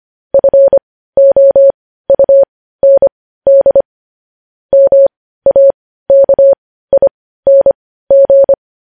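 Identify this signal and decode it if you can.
Morse code, heard as a keyed tone, about 25 wpm, spaced as for 15 wpm, 550 Hz. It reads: FOUND MAKING
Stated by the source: News headlines sent in Morse code at 15 wpm.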